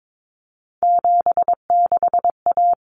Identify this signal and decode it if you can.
Morse code, heard as a keyed tone, about 22 wpm, 700 Hz. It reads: 76A
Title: Common Cut Numbers - 3-Digit 22wpm